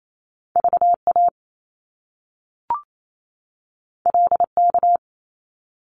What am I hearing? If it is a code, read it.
Morse code, heard as a keyed tone, about 28 wpm, 700 Hz. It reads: VA LK
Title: Top Two Letter Combinations in Words 28wpm